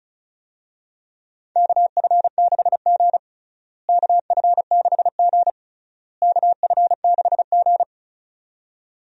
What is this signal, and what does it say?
Morse code, heard as a keyed tone, about 36 wpm, 700 Hz. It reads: KF6G KF6G KF6G
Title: USA Call Signs - 4 Characters - Repeated 3x - 35wpm